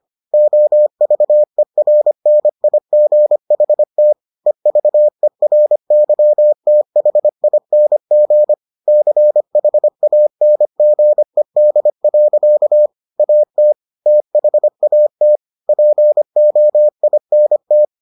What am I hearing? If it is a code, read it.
Morse code, heard as a keyed tone, about 25 wpm, 600 Hz. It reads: OVERNIGHT EVERYTHING CHANGED. AT THAT POINT